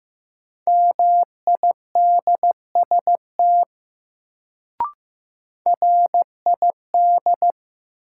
Morse code, heard as a keyed tone, about 15 wpm, 700 Hz. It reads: MIDST RID
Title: Sets of 4 Words - New Word Drill - 15wpm